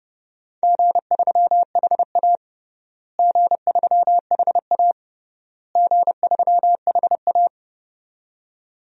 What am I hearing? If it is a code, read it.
Morse code, heard as a keyed tone, about 28 wpm, 700 Hz. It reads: G3HA G3HA G3HA
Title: DXCC-100 Call Signs - 4 Characters - Repeated 3x - 30wpm